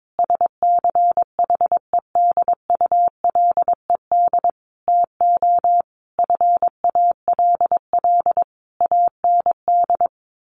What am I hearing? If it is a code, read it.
Morse code, heard as a keyed tone, about 22 wpm, 700 Hz. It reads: SCHEDULED TO FALL AND
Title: A Sound Of Thunder 22wpm